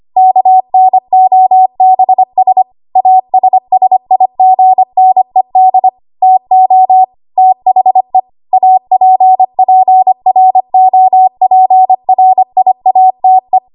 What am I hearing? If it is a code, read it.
Morse code, heard as a keyed tone, about 25 wpm, 750 Hz. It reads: KNOBS ASSIGNED TO THE APPROPRIATE
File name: Archive / 25 WPM / 25WPM